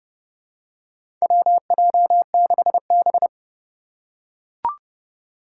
Morse code, heard as a keyed tone, about 30 wpm, 700 Hz. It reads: WJ6B
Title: Four Challenging Letter-Number Combinations 30wpm